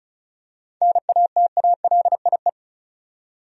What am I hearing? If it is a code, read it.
Morse code, heard as a keyed tone, about 34 wpm, 700 Hz. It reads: NATALIE
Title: USA Common Names - 35wpm